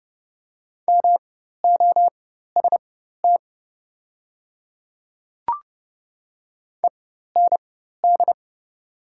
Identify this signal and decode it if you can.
Morse code, heard as a keyed tone, about 30 wpm, spaced as for 14 wpm, 690 Hz. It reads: MOST END
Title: 4x Character Spacing - Top 300 Words - 30wpm